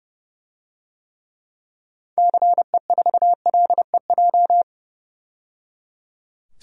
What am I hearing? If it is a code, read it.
Morse code, heard as a keyed tone, about 30 wpm, 700 Hz. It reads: CE4LEJ
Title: DXCC-100 Call Signs - 6 Characters - 30wpm